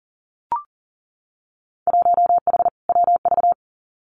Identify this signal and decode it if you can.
Morse code, heard as a keyed tone, about 42 wpm, 700 Hz. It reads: 1H WV
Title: Winter Field Day Exchanges 40wpm